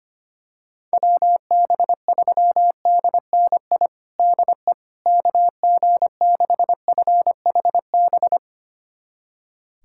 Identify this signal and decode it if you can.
Morse code, heard as a keyed tone, about 25 wpm, 700 Hz. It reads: WB3DNI DE KG6FHB
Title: USA Callsign DE Callsign - 6 Characters - 25wpm